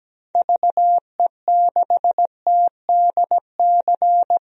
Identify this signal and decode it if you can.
Morse code, heard as a keyed tone, about 17 wpm, 700 Hz. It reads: VE6TDC
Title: DXCC-100 Call Signs - 6 Characters - 17wpm